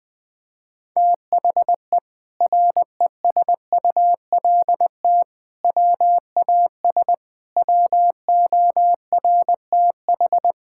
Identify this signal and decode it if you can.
Morse code, heard as a keyed tone, about 20 wpm, 700 Hz. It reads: THE RESULT WAS WORTH
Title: Sentences from Top 1000 Words - Review Part 5 - 20wpm